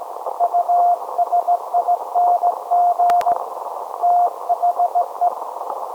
Morse code, heard as a keyed tone, about 17 wpm, 700 Hz. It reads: USING THE
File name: storage / message Grimeton SAQ